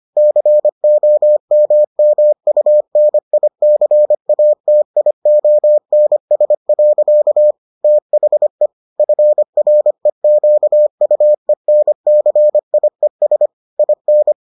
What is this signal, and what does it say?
Morse code, heard as a keyed tone, about 25 wpm, 600 Hz. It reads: COMMUNICATIONS. THE FREQUENCIES IN